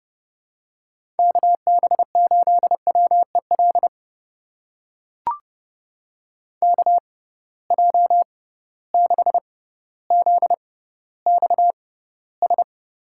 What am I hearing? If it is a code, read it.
Morse code, heard as a keyed tone, about 31 wpm, spaced as for 16 wpm, 700 Hz. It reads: KB8WEL KJ6ZXS
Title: USA Call Signs - 6 Characters - 6x Character Spacing - 30wpm